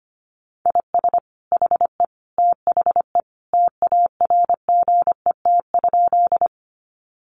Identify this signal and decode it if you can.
Morse code, heard as a keyed tone, about 25 wpm, 700 Hz. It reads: IS HE THE TARGET?
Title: Sentences from Top 1200 Words 25wpm